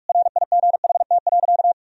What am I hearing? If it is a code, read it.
Morse code, heard as a keyed tone, about 43 wpm, 700 Hz. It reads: RIGHT.
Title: Sentences from Top 1000 Words - Review Part 3 - 45wpm